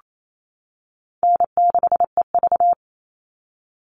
Morse code, heard as a keyed tone, about 28 wpm, 700 Hz. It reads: N6EV
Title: Call Signs - 28wpm